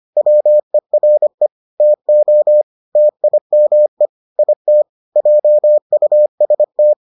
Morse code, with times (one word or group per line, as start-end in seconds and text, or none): 0.2-1.5 WERE
1.8-2.6 TO
3.0-4.1 TIME
4.4-4.8 IT
5.2-6.9 JUST